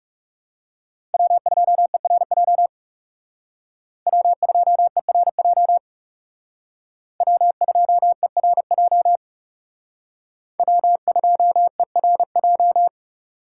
Morse code, heard as a keyed tone, about 38 wpm, 700 Hz. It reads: W2ERJ W2ERJ W2ERJ W2ERJ